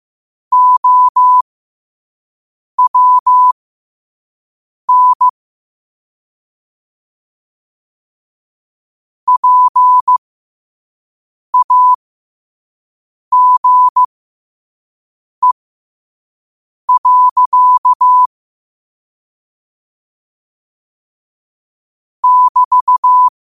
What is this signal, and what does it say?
Morse code, heard as a keyed tone, about 15 wpm, spaced as for 5 wpm, 1000 Hz. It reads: OWN PAGE. <BT>
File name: qotdfiles / 5wpm